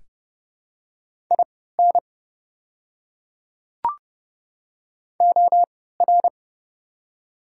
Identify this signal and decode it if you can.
Morse code, heard as a keyed tone, about 30 wpm, spaced as for 17 wpm, 700 Hz. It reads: IN OR
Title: 3x Character Spacing - Top 100 Words - 30wpm